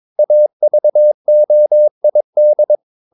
Morse code, heard as a keyed tone, about 22 wpm, 600 Hz. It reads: AVOID
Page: Romanen Never af Ken Follet på engelsk.